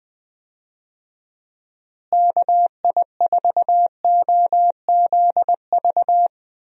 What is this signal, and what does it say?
Morse code, heard as a keyed tone, about 21 wpm, 700 Hz. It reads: KI4OZV
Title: USA Call Signs - 6 Characters - Repeated 3x - 20wpm